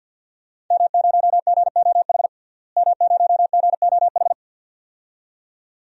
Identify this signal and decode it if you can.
Morse code, heard as a keyed tone, about 49 wpm, 700 Hz. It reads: M0GOH M0GOH
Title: DXCC-100 Call Signs - 5 Characters - Repeated 2x - 50wpm